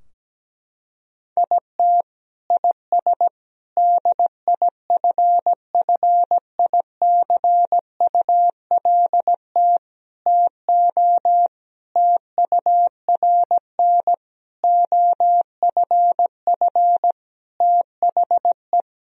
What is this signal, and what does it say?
Morse code, heard as a keyed tone, about 17 wpm, 700 Hz. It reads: IT IS DIFFICULT TO TURN OFF THE
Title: Sentences from Top 1000 Words - Review Part 3 - 17wpm